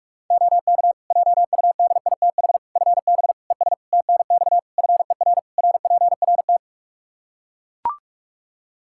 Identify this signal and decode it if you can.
Morse code, heard as a keyed tone, about 41 wpm, 700 Hz. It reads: OK JUDITH FB ES TNX FER RPRT
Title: Practice QSO - Exchange 2 - Info - 45wpm